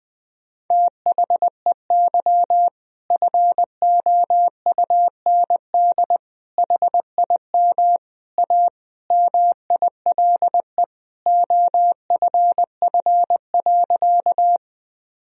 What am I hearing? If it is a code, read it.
Morse code, heard as a keyed tone, about 20 wpm, 700 Hz. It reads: THEY FOUND HIM A MILE OFF.
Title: Sentences from Top 1000 Words - Review Part 1 - 20wpm